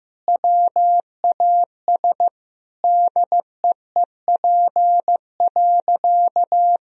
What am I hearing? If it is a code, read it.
Morse code, heard as a keyed tone, about 15 wpm, 700 Hz. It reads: WAS DEEP.
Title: Sentences from Top 1000 Words - Review Part 3 - 15wpm